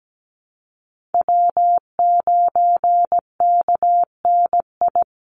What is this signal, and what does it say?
Morse code, heard as a keyed tone, about 17 wpm, 700 Hz. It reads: W9KNI
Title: Call Signs - 17wpm